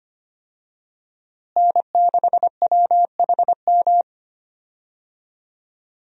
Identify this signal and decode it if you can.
Morse code, heard as a keyed tone, about 25 wpm, 700 Hz. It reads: N6WHM
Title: USA Call Signs - 5 Characters - Repeated 2x - 25wpm